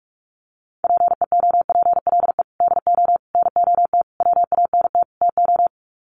Morse code, heard as a keyed tone, about 44 wpm, 700 Hz. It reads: PEOPLE DO NOT WANT TO
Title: Sets of 5 Words Encore2 45wpm